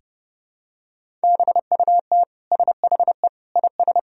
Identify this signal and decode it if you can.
Morse code, heard as a keyed tone, about 30 wpm, 700 Hz. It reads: BUT SHE IS
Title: Sentences from Top 400 Words and Abbreviations 30wpm